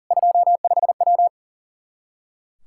Morse code, heard as a keyed tone, about 40 wpm, 700 Hz. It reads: 25W
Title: Practice QSO - Exchange 2 - Info - 40wpm